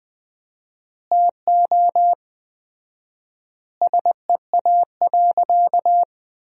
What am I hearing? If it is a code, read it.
Morse code, heard as a keyed tone, about 20 wpm, 700 Hz. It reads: TO SEA.
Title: 4x Word-Spacing - Sentences from Top 400 Words - 20wpm